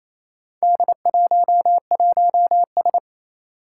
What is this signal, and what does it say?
Morse code, heard as a keyed tone, about 28 wpm, 700 Hz. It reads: D11S
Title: Four Challenging Letter-Number Combinations 28wpm